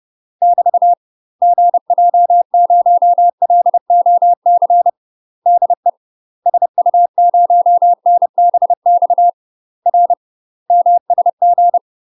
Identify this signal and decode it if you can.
Morse code, heard as a keyed tone, about 30 wpm, 700 Hz. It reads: <BT> GJ0LOC DE SU0NBX R MSG
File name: UK MASTER QSOs TRAINING / 30WPM / UK MASTER QSOs 0040